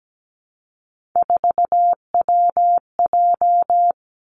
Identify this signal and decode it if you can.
Morse code, heard as a keyed tone, about 17 wpm, 700 Hz. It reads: 4WJ
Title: Three Challenging Letter-Number Combinations 17wpm